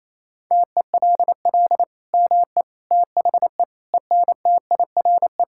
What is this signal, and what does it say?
Morse code, heard as a keyed tone, about 28 wpm, 700 Hz. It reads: TELL ME THE ENTIRE
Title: Sentences from Top 1000 Words - Review Part 5 - 28wpm